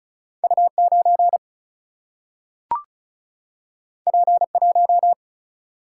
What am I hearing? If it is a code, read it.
Morse code, heard as a keyed tone, about 36 wpm, 700 Hz. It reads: U 9 P 1
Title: Single-Letter Single-Number 35wpm